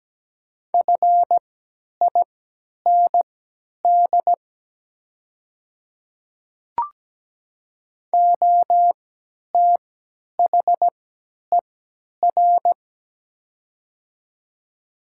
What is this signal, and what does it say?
Morse code, heard as a keyed tone, about 17 wpm, spaced as for 10 wpm, 700 Hz. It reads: FIND OTHER
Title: 3x Character Spacing - Top 100 Words - 17wpm